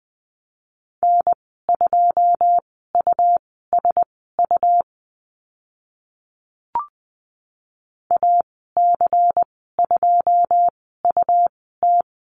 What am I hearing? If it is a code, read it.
Morse code, heard as a keyed tone, about 20 wpm, spaced as for 14 wpm, 700 Hz. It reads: N2USU AC2UT
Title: 128 Lesson 21g - Number 6 - Call Signs 20wpm